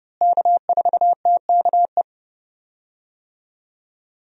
Morse code, heard as a keyed tone, about 29 wpm, 700 Hz. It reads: K4TKE
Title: USA Call Signs - 5 Characters - Repeated 3x - 30wpm